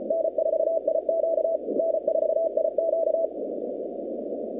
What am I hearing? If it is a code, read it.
Morse code, heard as a keyed tone, about 33 wpm, 650 Hz, with noise. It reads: N4IQ N4IQ
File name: contests / 2019-cw-wpx-kt5j-k5pi- -2330 0-47